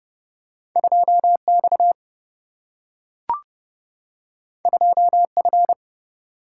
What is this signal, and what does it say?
Morse code, heard as a keyed tone, about 30 wpm, 700 Hz. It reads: 2 X 2 F
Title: Single-Letter Single-Number 30wpm